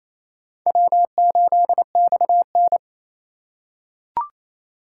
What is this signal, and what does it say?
Morse code, heard as a keyed tone, about 28 wpm, 700 Hz. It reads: W8XN
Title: USA Call Signs - 4 Characters - 8x Character Spacing - 28wpm